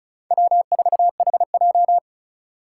Morse code, heard as a keyed tone, about 46 wpm, 700 Hz. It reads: W4HJ